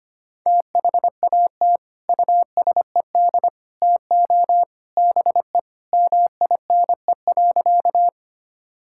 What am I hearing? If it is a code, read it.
Morse code, heard as a keyed tone, about 25 wpm, 700 Hz. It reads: THAT USED TO BE MINE.
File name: practice / 25wpm / Sentences with 3-Word Phrases - 25wpm